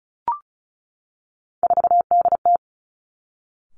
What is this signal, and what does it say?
Morse code, heard as a keyed tone, about 36 wpm, 700 Hz. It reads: 4DT
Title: Three Challenging Letter-Number Combinations 35wpm